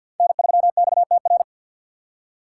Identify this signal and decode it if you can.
Morse code, heard as a keyed tone, about 48 wpm, 700 Hz. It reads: N3XTR
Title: USA Call Signs - 5 Characters - 6x Character Spacing - 50wpm